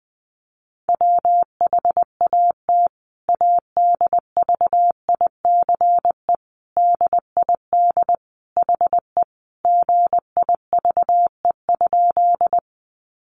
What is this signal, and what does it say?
Morse code, heard as a keyed tone, about 20 wpm, 700 Hz. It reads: WHAT ADVICE DID HE GIVE?
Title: Sentences from Top 1100 Words 20wpm